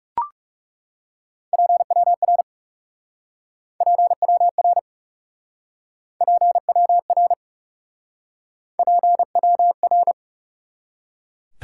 Morse code, heard as a keyed tone, about 38 wpm, 700 Hz. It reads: PWR PWR PWR PWR